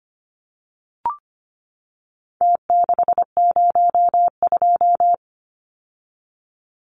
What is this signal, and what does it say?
Morse code, heard as a keyed tone, about 25 wpm, 700 Hz. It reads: T602
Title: Numbers - Contest Style - 4-Digit 25wpm